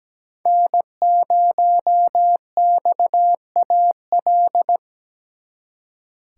Morse code, heard as a keyed tone, about 20 wpm, 700 Hz. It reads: N0XAL
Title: Speed Racing - USA 1x3 Call Signs 17wpm